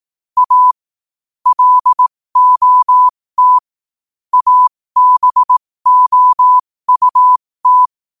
Morse code, heard as a keyed tone, about 18 wpm, spaced as for 15 wpm, 1000 Hz. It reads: A LOT ABOUT
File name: qotdfiles / 15wpm